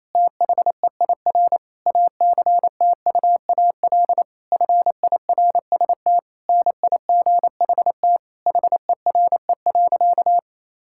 Morse code, heard as a keyed tone, about 28 wpm, 700 Hz. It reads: THEIR ACTUAL FIRST NIGHT HERE.
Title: Sentences from Top 1000 Words - Review Part 5 - 28wpm